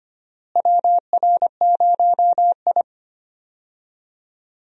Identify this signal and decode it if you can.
Morse code, heard as a keyed tone, about 25 wpm, 700 Hz. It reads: WR0I
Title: USA Call Signs - 4 Characters - Repeated 3x - 25wpm